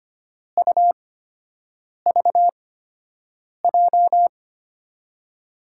Morse code, heard as a keyed tone, about 25 wpm, spaced as for 7 wpm, 700 Hz. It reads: UVJ
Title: USA Call Signs - 6 Characters - 8x Character Spacing - 25wpm